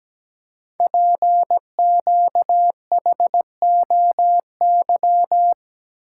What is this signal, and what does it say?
Morse code, heard as a keyed tone, about 17 wpm, 700 Hz. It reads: PQHOY
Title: Code Groups - 5 Letters - 17wpm